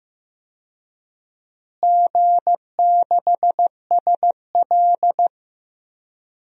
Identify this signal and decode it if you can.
Morse code, heard as a keyed tone, about 15 wpm, 700 Hz. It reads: G6SL
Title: DXCC-100 Call Signs - 4 Characters - Repeated 2x - 15wpm